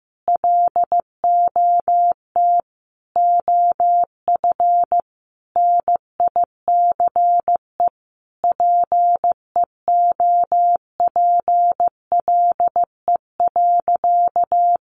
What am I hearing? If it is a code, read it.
Morse code, heard as a keyed tone, about 15 wpm, 700 Hz. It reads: LOT OF NICE PEOPLE.
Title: Sentences from Top 1100 Words 15wpm